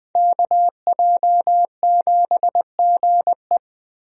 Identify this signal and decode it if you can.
Morse code, heard as a keyed tone, about 20 wpm, 700 Hz. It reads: KJ7GE
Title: USA Call Signs - 5 Characters - Repeated 3x - 20wpm